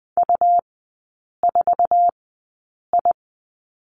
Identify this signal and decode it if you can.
Morse code, heard as a keyed tone, about 20 wpm, spaced as for 8 wpm, 700 Hz. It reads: U4I
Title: 100 Lesson 17e - Letter u - Three Characters 20wpm